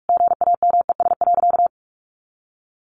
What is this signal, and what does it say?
Morse code, heard as a keyed tone, about 44 wpm, 700 Hz. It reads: GAMES.
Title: Additional Easy Phrases 45wpm